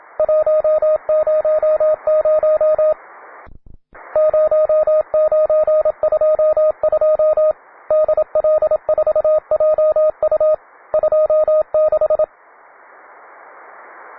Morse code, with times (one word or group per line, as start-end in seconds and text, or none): 0.2-2.9 100
4.2-7.5 0922
7.9-10.5 DL4JU
10.9-12.2 26